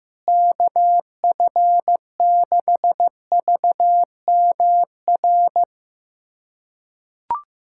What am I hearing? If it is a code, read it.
Morse code, heard as a keyed tone, about 15 wpm, 700 Hz. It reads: KF6VMR
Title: USA Callsign DE Callsign - 6 Characters - 15wpm